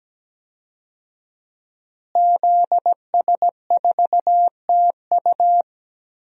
Morse code, heard as a keyed tone, about 17 wpm, 700 Hz. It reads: ZS4TU
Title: DXCC-100 Call Signs - 5 Characters - Repeated 3x - 17wpm